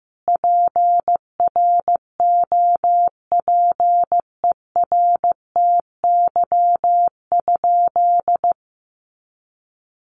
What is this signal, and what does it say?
Morse code, heard as a keyed tone, about 15 wpm, 700 Hz. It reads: PROPERTY?
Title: Sentences from Top 1500 Words 15wpm